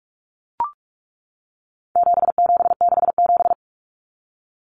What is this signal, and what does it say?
Morse code, heard as a keyed tone, about 44 wpm, 700 Hz. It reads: 7767
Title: Numbers - 4-Digit 45wpm